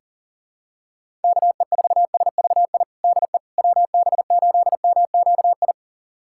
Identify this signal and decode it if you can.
Morse code, heard as a keyed tone, about 37 wpm, 700 Hz. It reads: KE4SVI DE WB8MQI
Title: USA Callsign DE Callsign - 6 Characters - 40wpm